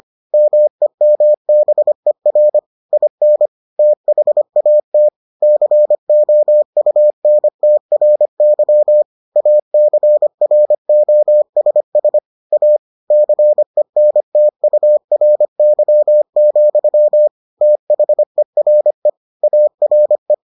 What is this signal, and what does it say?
Morse code, heard as a keyed tone, about 25 wpm, 600 Hz. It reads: MEMBER IN THAT COUNTRY ACROSS A CENTURY, THERE ARE